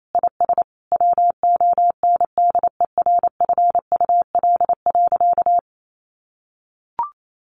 Morse code, heard as a keyed tone, about 28 wpm, 700 Hz. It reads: IS WONDERFUL.
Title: Sentences from Top 1100 Words 28wpm